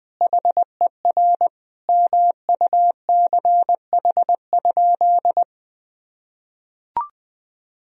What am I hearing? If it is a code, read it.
Morse code, heard as a keyed tone, about 20 wpm, 700 Hz. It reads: HER MUCH?
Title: Sentences from Top 1000 Words - Review Part 1 - 20wpm